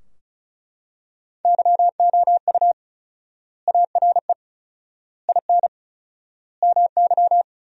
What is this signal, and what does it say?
Morse code, heard as a keyed tone, about 36 wpm, 700 Hz. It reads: YOU ARE IN MY
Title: 4x Word-Spacing - Sentences from Top 500 Words - 35wpm